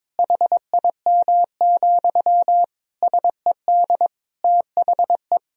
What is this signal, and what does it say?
Morse code, heard as a keyed tone, about 22 wpm, 700 Hz. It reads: HIM, SED THE
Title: Sentences from Top 400 Words and Abbreviations 22wpm